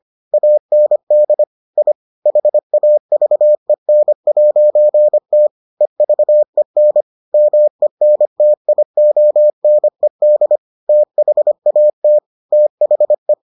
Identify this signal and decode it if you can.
Morse code, heard as a keyed tone, about 25 wpm, 600 Hz. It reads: AND I HAVEN'T EVEN MENTIONED THAT THE